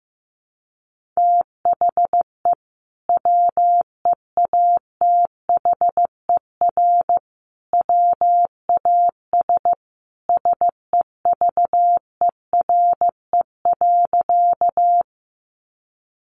Morse code, heard as a keyed tone, about 15 wpm, 700 Hz. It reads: THE WEATHER WAS SEVERE.
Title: Sentences from Top 1400 Words 15wpm